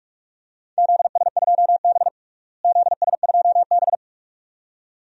Morse code, heard as a keyed tone, about 45 wpm, 700 Hz. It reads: ZS2B ZS2B
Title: DXCC-100 Call Signs - 4 Characters - Repeated 2x - 45wpm